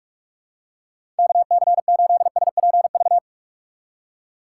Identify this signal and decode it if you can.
Morse code, heard as a keyed tone, about 46 wpm, 700 Hz. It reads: KC8SPV
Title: USA Call Signs - 3x Character Spacing - 45wpm